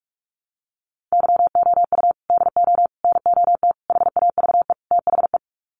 Morse code, heard as a keyed tone, about 44 wpm, 700 Hz. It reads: YOU DO NOT HAVE THE
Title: Sets of 5 Words 45wpm